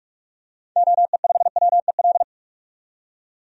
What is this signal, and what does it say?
Morse code, heard as a keyed tone, about 45 wpm, 700 Hz. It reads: OE5WEL
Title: DXCC-100 Call Signs - 6 Characters - 45wpm